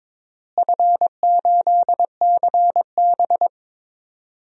Morse code, heard as a keyed tone, about 22 wpm, 700 Hz. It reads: F8CB
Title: DXCC-100 Call Signs - 4 Characters - Repeated 2x - 22wpm